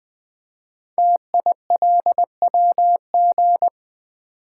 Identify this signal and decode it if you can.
Morse code, heard as a keyed tone, about 20 wpm, 700 Hz. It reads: TILWG
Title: Code Groups - 5 Letters - 20wpm